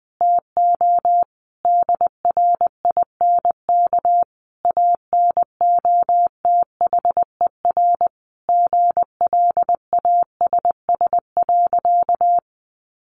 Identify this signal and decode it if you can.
Morse code, heard as a keyed tone, about 20 wpm, 700 Hz. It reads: TO DRINK ANOTHER GLASS.
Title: Sentences from Top 1500 Words 20wpm